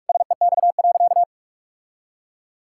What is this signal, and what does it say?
Morse code, heard as a keyed tone, about 46 wpm, 700 Hz. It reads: SEX.
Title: Sentences from Top 1600 Words 45wpm